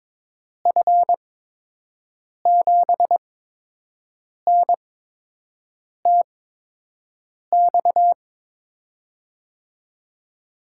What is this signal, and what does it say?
Morse code, heard as a keyed tone, about 23 wpm, spaced as for 6 wpm, 700 Hz. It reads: F7NTX
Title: USA Call Signs - 6 Characters - 8x Character Spacing - 22wpm